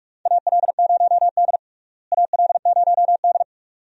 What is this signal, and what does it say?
Morse code, heard as a keyed tone, about 46 wpm, 700 Hz. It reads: AL0D AL0D
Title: USA Call Signs - 4 Characters - Repeated 2x - 45wpm